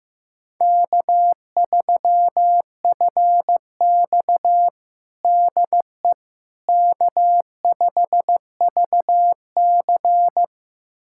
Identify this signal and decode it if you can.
Morse code, heard as a keyed tone, about 15 wpm, 700 Hz. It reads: K3FX DE K5VC
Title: USA Callsign DE Callsign - 4 Characters - 15wpm